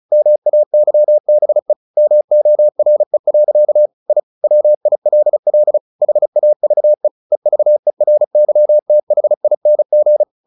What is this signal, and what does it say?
Morse code, heard as a keyed tone, about 35 wpm, 600 Hz. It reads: MAYBE MORE. I WILL HAVE EVERYTHING
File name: the old man and the sea 2 35wpm